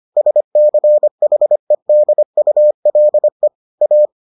Morse code, heard as a keyed tone, about 25 wpm, 600 Hz. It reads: SCHEDULE A